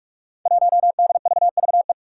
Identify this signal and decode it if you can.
Morse code, heard as a keyed tone, about 46 wpm, 700 Hz. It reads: 1DVVE
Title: Five Challenging Letter-Number Combinations 45wpm